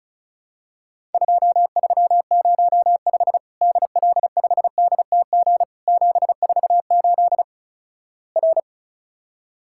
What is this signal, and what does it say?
Morse code, heard as a keyed tone, about 36 wpm, 700 Hz. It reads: 2305 DL5DTG 748 R
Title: Worked All Europe DX Contest - QTC Exchange - 35wpm